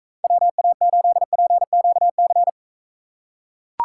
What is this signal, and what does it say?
Morse code, heard as a keyed tone, about 44 wpm, 700 Hz. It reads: WA8PQC